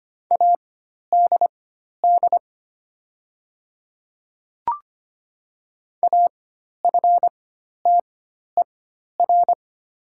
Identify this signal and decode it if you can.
Morse code, heard as a keyed tone, about 25 wpm, spaced as for 12 wpm, 700 Hz. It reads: ADD AFTER
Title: 4x Character Spacing - Top 200 Words - 25wpm